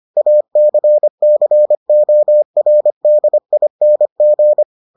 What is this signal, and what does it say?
Morse code, heard as a keyed tone, about 25 wpm, 600 Hz. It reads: ACCORDING